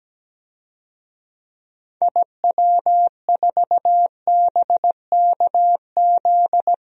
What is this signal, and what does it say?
Morse code, heard as a keyed tone, about 17 wpm, 700 Hz. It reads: IW4BKZ
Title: DXCC-100 Call Signs - 6 Characters - 17wpm